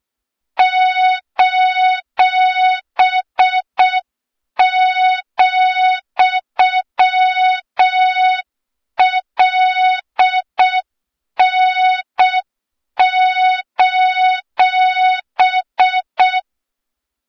Morse code, heard as a keyed tone, about 6 wpm, 3700 Hz, with noise. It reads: :,LN:
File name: Morze-HA2MN / hang / 23 GYAKORLAT